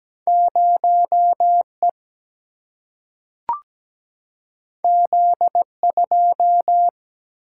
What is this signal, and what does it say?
Morse code, heard as a keyed tone, about 17 wpm, 700 Hz. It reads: 0E Z2
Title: Single-Letter Single-Number 17wpm